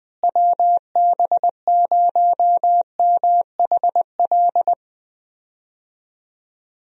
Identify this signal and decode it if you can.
Morse code, heard as a keyed tone, about 21 wpm, 700 Hz. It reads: WB0MHL
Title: USA Call Signs - 6 Characters - Repeated 2x - 20wpm